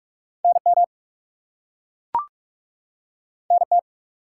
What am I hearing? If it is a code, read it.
Morse code, heard as a keyed tone, about 43 wpm, 700 Hz. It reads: NM NT
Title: US States and Canadian Provinces 45wpm